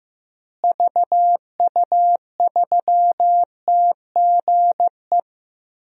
Morse code, heard as a keyed tone, about 15 wpm, 700 Hz. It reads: VU3TGE
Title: DXCC-100 Call Signs - 6 Characters - Repeated 2x - 15wpm